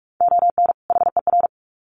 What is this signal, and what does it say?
Morse code, heard as a keyed tone, about 44 wpm, 700 Hz. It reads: ON HER
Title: Sets of 2 Words 45wpm